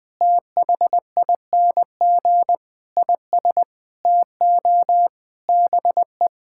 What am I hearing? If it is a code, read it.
Morse code, heard as a keyed tone, about 21 wpm, 700 Hz. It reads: THING IS TO BE
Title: Sentences with 4-Word Phrases - 20wpm